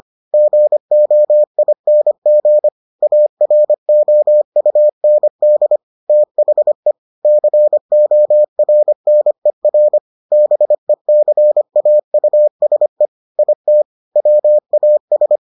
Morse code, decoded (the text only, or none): GOING AROUND THE CORNER BECAUSE IT WAS